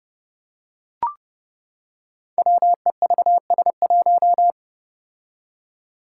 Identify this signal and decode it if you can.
Morse code, heard as a keyed tone, about 30 wpm, 700 Hz. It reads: WEVS1
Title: Five Challenging Letter-Number Combinations 30wpm